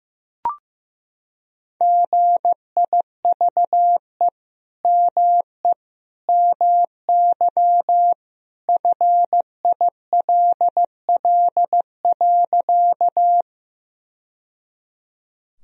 Morse code, heard as a keyed tone, about 15 wpm, 700 Hz. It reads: GIVE ME MY FILL.
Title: Sentences from Top 1000 Words - Review Part 3 - 15wpm